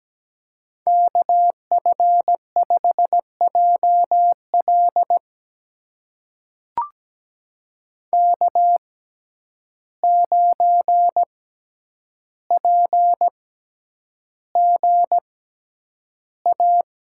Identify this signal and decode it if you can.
Morse code, heard as a keyed tone, about 17 wpm, spaced as for 9 wpm, 700 Hz. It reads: KF5JL K9PGA
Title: USA Call Signs - 5 Characters - 6x Character Spacing - 17wpm